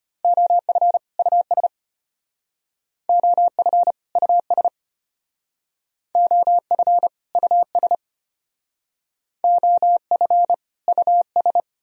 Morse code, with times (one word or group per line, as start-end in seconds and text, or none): 0.2-1.0 OF
1.2-1.7 US
3.1-3.9 OF
4.2-4.7 US
6.2-7.1 OF
7.4-7.9 US
9.4-10.5 OF
10.9-11.6 US